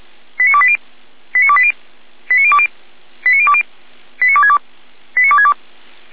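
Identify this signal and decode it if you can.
Five-tone selective call (ZVEI), heard as quick runs of five tones, tones of about 70 ms. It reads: 89280 89280 89020 89020 89262 89262
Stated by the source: ZVEI five-tone call 89230 received by SDR, Feuerwehr Orlamünde.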